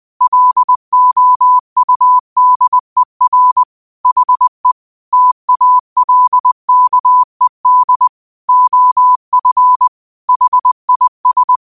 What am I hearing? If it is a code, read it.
Morse code, heard as a keyed tone, about 20 wpm, 1000 Hz. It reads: LOUDER HE TALKED OF HIS
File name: qotdfiles / 20wpm